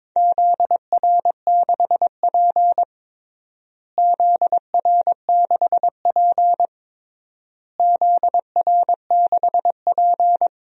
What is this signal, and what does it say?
Morse code, heard as a keyed tone, about 21 wpm, 700 Hz. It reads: ZR6P ZR6P ZR6P
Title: DXCC-100 Call Signs - 4 Characters - Repeated 3x - 22wpm